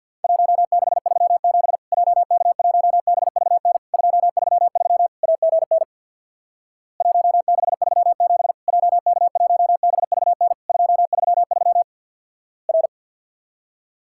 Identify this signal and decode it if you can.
Morse code, heard as a keyed tone, about 50 wpm, 700 Hz. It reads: 1637 JK1BVN 233 AGN 1637 JK1BVN 233 R
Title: Worked All Europe DX Contest - QTC Exchange with Repeats - 50wpm